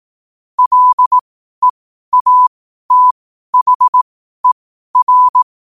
Morse code, heard as a keyed tone, about 18 wpm, spaced as for 12 wpm, 1000 Hz. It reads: LEATHER
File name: qotdfiles / 12wpm